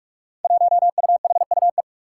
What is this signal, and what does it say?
Morse code, heard as a keyed tone, about 46 wpm, 700 Hz. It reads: 1UHUE
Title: Five Challenging Letter-Number Combinations 45wpm